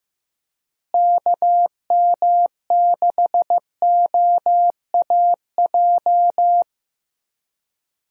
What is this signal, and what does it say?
Morse code, heard as a keyed tone, about 15 wpm, 700 Hz. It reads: KM6OAJ
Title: USA Call Signs - 6 Characters - Repeated 3x - 15wpm